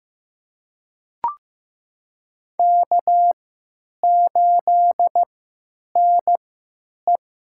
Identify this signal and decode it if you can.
Morse code, heard as a keyed tone, about 15 wpm, spaced as for 9 wpm, 700 Hz. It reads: K8NE
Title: USA Call Signs - 4 Characters - 3x Character Spacing - 15wpm